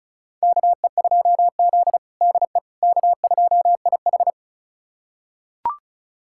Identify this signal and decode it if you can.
Morse code, heard as a keyed tone, about 36 wpm, 700 Hz. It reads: KE2Z DE K2IH
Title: USA Callsign DE Callsign - 4 Characters - 35wpm